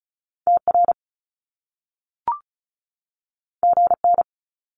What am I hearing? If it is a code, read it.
Morse code, heard as a keyed tone, about 36 wpm, 700 Hz. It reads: TR GN
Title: Top Two Letter Combinations in Words 35wpm